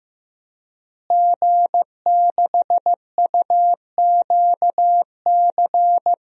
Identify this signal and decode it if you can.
Morse code, heard as a keyed tone, about 15 wpm, 700 Hz. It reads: G6UQC
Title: DXCC-100 Call Signs - 5 Characters - Repeated 2x - 15wpm